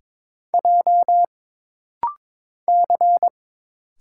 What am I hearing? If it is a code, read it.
Morse code, heard as a keyed tone, about 22 wpm, 700 Hz. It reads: J C
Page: This rapid-fire timing is geared to push your ICR (Instant Character Recognition) skills.